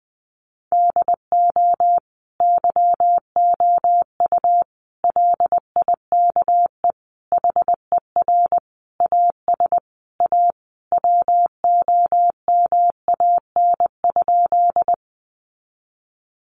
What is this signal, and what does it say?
Morse code, heard as a keyed tone, about 20 wpm, 700 Hz. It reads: DO YOU LIKE HER AS A WOMAN?
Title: Sentences from Top 1100 Words 20wpm